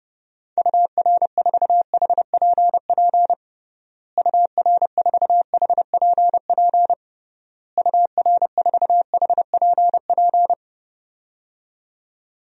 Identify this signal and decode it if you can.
Morse code, heard as a keyed tone, about 31 wpm, 700 Hz. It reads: UR4HPP UR4HPP UR4HPP
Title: DXCC-100 Call Signs - 6 Characters - Repeated 3x - 30wpm